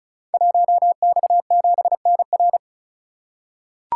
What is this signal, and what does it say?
Morse code, heard as a keyed tone, about 36 wpm, 700 Hz. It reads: 1X7NR
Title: Code Groups - 5 Letter:Numbers - 35wpm